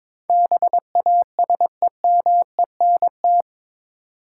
Morse code, heard as a keyed tone, about 22 wpm, 700 Hz. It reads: BASEMENT
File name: practice / 22wpm / Eight Letter Words - 22wpm